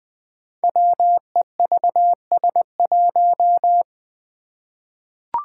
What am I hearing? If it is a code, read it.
Morse code, heard as a keyed tone, about 20 wpm, 700 Hz. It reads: WEVS1
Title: Five Challenging Letter-Number Combinations 20wpm